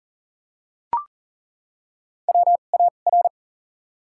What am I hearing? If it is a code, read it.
Morse code, heard as a keyed tone, about 40 wpm, 700 Hz. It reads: W A R
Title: 2x Character Spacing - Top 500 Words - 40wpm